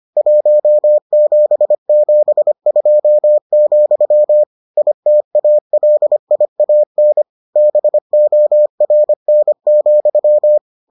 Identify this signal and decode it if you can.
Morse code, heard as a keyed tone, about 25 wpm, 600 Hz. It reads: 1772, ITALIAN BORN,